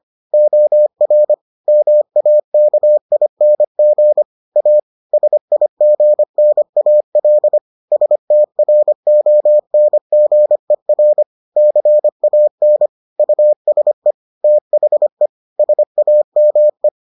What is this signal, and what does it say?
Morse code, heard as a keyed tone, about 25 wpm, 600 Hz. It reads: OR MAKING A SIGNAL STRONGER CAN USE THE SAME